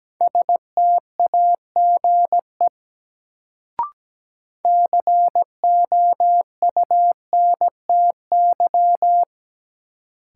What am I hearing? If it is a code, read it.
Morse code, heard as a keyed tone, about 17 wpm, 700 Hz. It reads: STAGE COUNTY
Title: Sentences from Top 1200 Words - New Word Drill - 17wpm